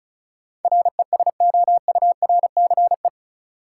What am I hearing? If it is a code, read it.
Morse code, heard as a keyed tone, about 34 wpm, 700 Hz. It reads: RESOURCE
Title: Sentences from Top 1300 Words - New Word Drill Alt - 35wpm